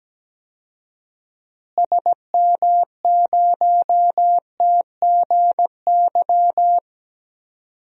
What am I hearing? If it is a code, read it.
Morse code, heard as a keyed tone, about 17 wpm, 700 Hz. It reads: SM0TGY
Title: DXCC-100 Call Signs - 6 Characters - Repeated 2x - 17wpm